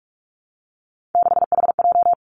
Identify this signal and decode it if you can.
Morse code, heard as a keyed tone, about 46 wpm, 700 Hz. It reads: 6HJ
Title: Three Challenging Letter-Number Combinations 45wpm